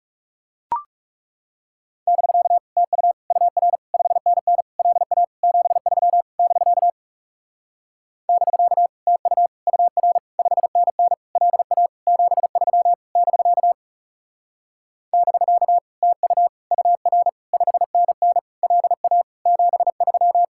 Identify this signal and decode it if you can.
Morse code, heard as a keyed tone, about 40 wpm, 700 Hz. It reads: <BK> TU UR 5NN LA 73 <BK> <BK> TU UR 5NN LA 73 <BK> <BK> TU UR 5NN LA 73